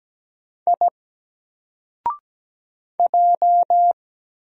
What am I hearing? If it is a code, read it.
Morse code, heard as a keyed tone, about 17 wpm, 700 Hz. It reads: I J
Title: Single Character (Letter:Number:Punctuation:Prosign) 17wpm